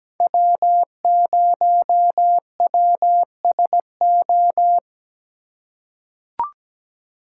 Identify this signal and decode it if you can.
Morse code, heard as a keyed tone, about 17 wpm, 700 Hz. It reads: W0WSO
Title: USA Call Signs - 5 Characters - 3x Character Spacing - 17wpm